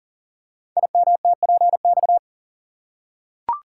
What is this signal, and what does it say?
Morse code, heard as a keyed tone, about 39 wpm, 700 Hz. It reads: IMTPX
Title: Code Groups - 5 Letters - 40wpm